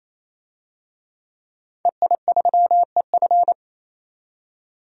Morse code, heard as a keyed tone, about 27 wpm, 700 Hz. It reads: EI3EF
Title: DXCC-100 Call Signs - 5 Characters - 28wpm